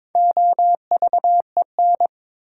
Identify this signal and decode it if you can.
Morse code, heard as a keyed tone, about 22 wpm, 700 Hz. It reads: OVEN
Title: Four Letter Words - 22wpm